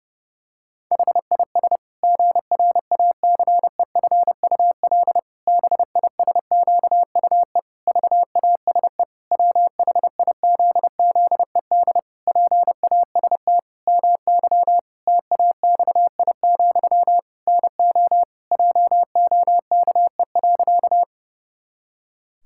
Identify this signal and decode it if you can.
Morse code, heard as a keyed tone, about 30 wpm, 700 Hz. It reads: HIS GRACEFUL BISQUE VASE WHIZZED PAST MY TAXI, NO JOKE.
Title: Pangram Challenge v2 - 30wpm